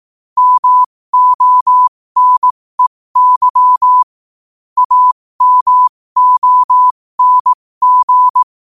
Morse code, heard as a keyed tone, about 18 wpm, spaced as for 15 wpm, 1000 Hz. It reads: MONEY AMONG
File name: qotdfiles / 15wpm